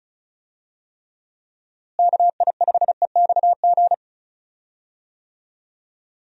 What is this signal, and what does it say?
Morse code, heard as a keyed tone, about 36 wpm, 700 Hz. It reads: KI5EXG
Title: USA Call Signs - 6 Characters - Repeated 2x - 35wpm